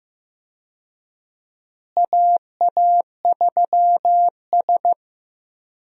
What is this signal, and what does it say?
Morse code, heard as a keyed tone, about 15 wpm, 700 Hz. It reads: AA3S
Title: USA Call Signs - 4 Characters - Repeated 2x - 15wpm